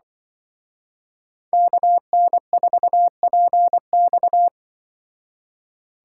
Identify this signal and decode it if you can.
Morse code, heard as a keyed tone, about 25 wpm, 700 Hz. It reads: KN4PX
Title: Speed Racing - USA Call Signs 20wpm